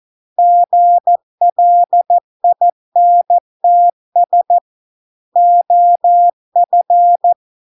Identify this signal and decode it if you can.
Morse code, heard as a keyed tone, about 14 wpm, 700 Hz. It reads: GLINTS OF